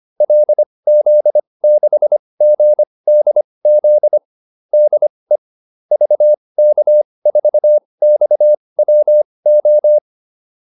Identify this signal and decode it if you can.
Morse code, heard as a keyed tone, about 25 wpm, spaced as for 20 wpm, 600 Hz. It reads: LZ6GDZ DE VK4XWO
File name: QSO407 20